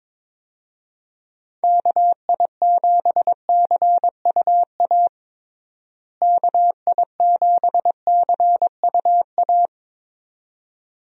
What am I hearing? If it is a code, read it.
Morse code, heard as a keyed tone, about 23 wpm, 700 Hz. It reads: KI7CUA KI7CUA
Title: USA Call Signs - 6 Characters - Repeated 2x - 22wpm